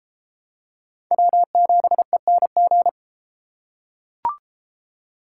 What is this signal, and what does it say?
Morse code, heard as a keyed tone, about 32 wpm, 700 Hz. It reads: W7ENG